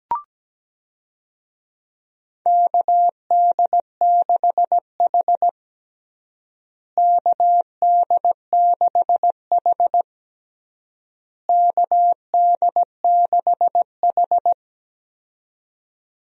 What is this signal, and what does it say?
Morse code, heard as a keyed tone, about 17 wpm, 700 Hz. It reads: KD6H KD6H KD6H
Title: USA Call Signs - 4 Characters - Repeated 3x - 17wpm